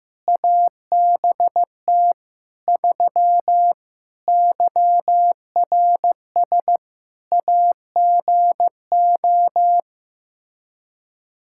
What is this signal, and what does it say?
Morse code, heard as a keyed tone, about 15 wpm, 700 Hz. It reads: ABT 3 YRS AGO
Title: Practice QSO - Exchange 3 - More Info - 15wpm